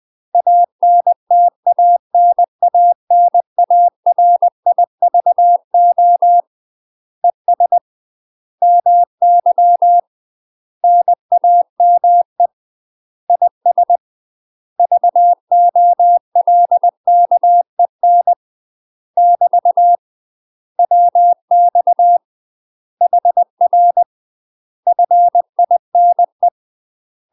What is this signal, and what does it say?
Morse code, heard as a keyed tone, about 20 wpm, 700 Hz. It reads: ANTANANARIVO ES MY NAME IS VOLKEN <BT> WX HR FINE
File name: UK MASTER QSOs TRAINING / 20WPM / UK MASTER QSOs 0062